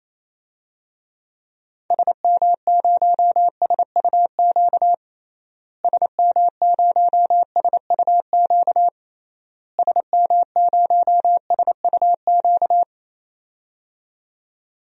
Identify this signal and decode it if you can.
Morse code, heard as a keyed tone, about 28 wpm, 700 Hz. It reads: SM0SUQ SM0SUQ SM0SUQ
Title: DXCC-100 Call Signs - 6 Characters - Repeated 3x - 28wpm